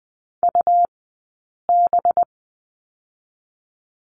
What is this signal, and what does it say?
Morse code, heard as a keyed tone, about 20 wpm, 700 Hz. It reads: U B
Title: 183 Lesson 30d - Letter b - Two Characters 20wpm